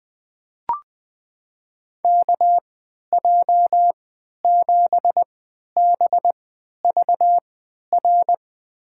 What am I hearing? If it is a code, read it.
Morse code, heard as a keyed tone, about 20 wpm, spaced as for 11 wpm, 700 Hz. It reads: KJ7BVR
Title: USA Call Signs - 6 Characters - 3x Character Spacing - 20wpm